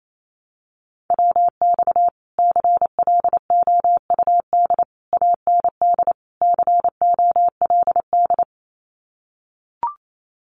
Morse code, heard as a keyed tone, about 28 wpm, 700 Hz. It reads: WX CLOUD AND COLD
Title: Common QSO Phrases Encore 28wpm